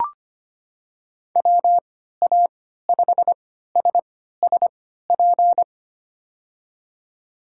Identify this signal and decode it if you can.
Morse code, heard as a keyed tone, about 25 wpm, spaced as for 14 wpm, 700 Hz. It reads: WA5SSP
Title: USA Call Signs - 3x Character Spacing - 25wpm